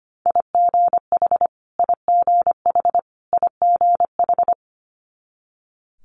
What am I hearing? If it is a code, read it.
Morse code, heard as a keyed tone, about 25 wpm, 700 Hz. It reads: IGH IGH IGH
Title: Multi-letter Phonemes Repeated 3x 25wpm